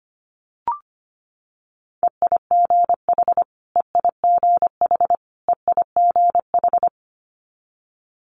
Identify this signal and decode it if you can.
Morse code, heard as a keyed tone, about 25 wpm, 700 Hz. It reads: EIGH EIGH EIGH
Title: Multi-letter Phonemes Repeated 3x 25wpm